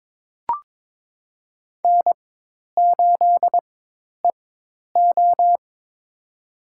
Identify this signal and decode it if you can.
Morse code, heard as a keyed tone, about 22 wpm, spaced as for 10 wpm, 700 Hz. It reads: N8EO
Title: USA Call Signs - 4 Characters - 4x Character Spacing - 22wpm